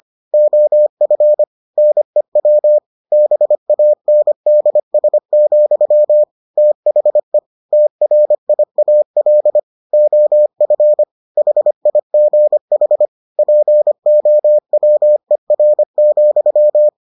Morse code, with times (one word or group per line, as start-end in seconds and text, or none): 0.3-1.4 OF
1.8-2.8 NEW
3.1-6.2 BANDS,
6.6-7.4 THE
7.7-9.6 TRIAL
9.9-11.0 OF
11.4-13.1 HIGH
13.4-16.9 POWER,